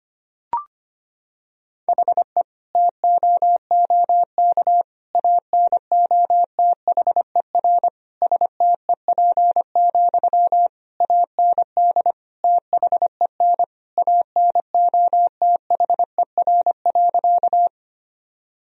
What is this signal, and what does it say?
Morse code, heard as a keyed tone, about 25 wpm, 700 Hz. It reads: HE TOOK ANOTHER STEP, AND THEN ANOTHER.
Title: Sentences from Top 1000 Words - Review Part 2 - 25wpm